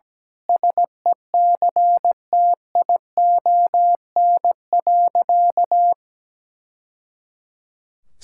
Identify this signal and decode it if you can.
Morse code, heard as a keyed tone, about 17 wpm, 700 Hz. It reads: SECTION.
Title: Sentences from Top 1000 Words - Review Part 5 - 17wpm